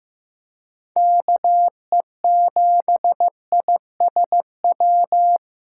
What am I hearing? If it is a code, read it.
Morse code, heard as a keyed tone, about 15 wpm, 700 Hz. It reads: KE7ISW
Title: USA Call Signs - 6 Characters - 4x Character Spacing - 15wpm